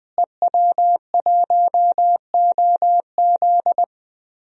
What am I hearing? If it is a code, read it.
Morse code, heard as a keyed tone, about 20 wpm, 700 Hz. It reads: EW1OZ
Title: DXCC-100 Call Signs - 5 Characters - Repeated 3x - 20wpm